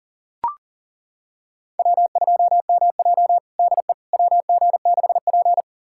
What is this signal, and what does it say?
Morse code, heard as a keyed tone, about 40 wpm, 700 Hz. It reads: W2MJ DE WG6P
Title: USA Callsign DE Callsign - 4 Characters - 40wpm